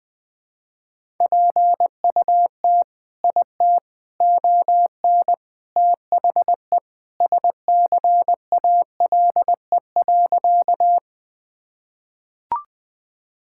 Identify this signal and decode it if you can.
Morse code, heard as a keyed tone, about 20 wpm, 700 Hz. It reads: PUT IT ON THE SCALE.
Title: Sentences from Top 1000 Words - Review Part 5 - 20wpm